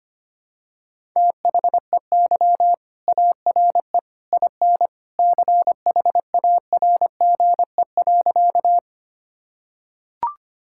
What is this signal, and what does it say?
Morse code, heard as a keyed tone, about 25 wpm, 700 Hz. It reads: THEY ARE IN CHARGE.
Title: Sentences from Top 1000 Words - Review Part 6 - 25wpm